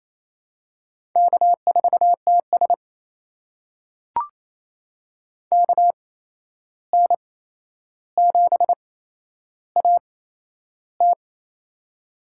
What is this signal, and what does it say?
Morse code, heard as a keyed tone, about 27 wpm, spaced as for 11 wpm, 700 Hz. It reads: K4TS KN7AT
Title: USA Call Signs - 8x Character Spacing - 28wpm